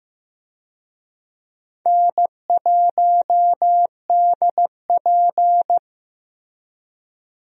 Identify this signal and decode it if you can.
Morse code, heard as a keyed tone, about 15 wpm, 700 Hz. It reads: N1DP
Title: USA Call Signs - 4 Characters - Repeated 2x - 15wpm